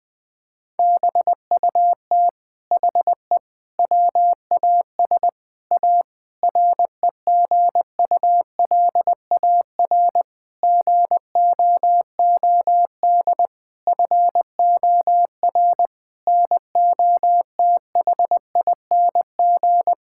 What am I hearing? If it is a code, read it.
Morse code, heard as a keyed tone, about 20 wpm, 700 Hz. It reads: BUT HE WAS A REGULAR GOOD FOR NOTHING
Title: Sentences from Top 1000 Words - Review Part 6 - 20wpm